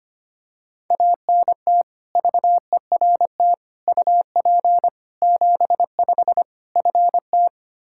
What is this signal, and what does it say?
Morse code, heard as a keyed tone, about 25 wpm, 700 Hz. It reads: ANT VERT UP 75 FT
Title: K1USN SST QSO Practice 25wpm